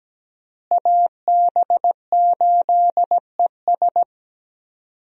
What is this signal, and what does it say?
Morse code, heard as a keyed tone, about 17 wpm, 700 Hz. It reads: AB8ES
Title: USA Call Signs - 5 Characters - Repeated 2x - 17wpm